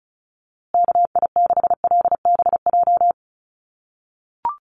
Morse code, heard as a keyed tone, about 34 wpm, 700 Hz. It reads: KI6LBJ
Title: USA Call Signs Repeated 2x 35wpm